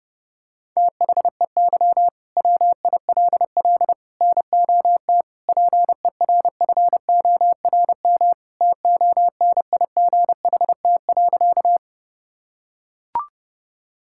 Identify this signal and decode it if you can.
Morse code, heard as a keyed tone, about 30 wpm, 700 Hz. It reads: THEY WILL NOT PERFORM TONIGHT.
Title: Sentences from Top 1000 Words - Review Part 5 - 30wpm